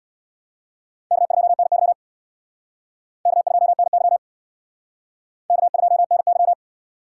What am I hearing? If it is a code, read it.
Morse code, heard as a keyed tone, about 61 wpm, 700 Hz. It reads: K2NY K2NY K2NY
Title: Speed Racing - USA 1x2 Call Signs 50wpm